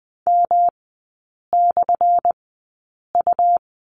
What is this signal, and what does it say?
Morse code, heard as a keyed tone, about 20 wpm, spaced as for 8 wpm, 700 Hz. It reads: M/U
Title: 178 Lesson 29e - Slash - Three Characters 20wpm